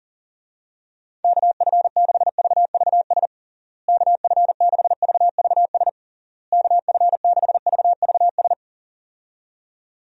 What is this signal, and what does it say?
Morse code, heard as a keyed tone, about 37 wpm, 700 Hz. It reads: KF6VVS KF6VVS KF6VVS
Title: USA Call Signs - 6 Characters - Repeated 3x - 40wpm